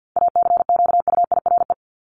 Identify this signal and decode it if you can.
Morse code, heard as a keyed tone, about 50 wpm, 700 Hz. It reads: ACQUIRE
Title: Top 1501-1750 Words 50wpm